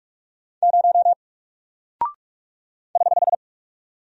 Morse code, heard as a keyed tone, about 44 wpm, 700 Hz. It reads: 0 <HH>
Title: Single Character (Letter:Number:Punctuation:Prosign) Alt 45wpm